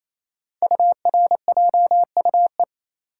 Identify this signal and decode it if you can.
Morse code, heard as a keyed tone, about 27 wpm, 700 Hz. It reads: URJUE
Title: Code Groups - 5 Letters - 28wpm